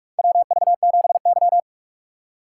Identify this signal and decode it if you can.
Morse code, heard as a keyed tone, about 46 wpm, 700 Hz. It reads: WV7Y
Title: USA Call Signs - 3x Character Spacing - 45wpm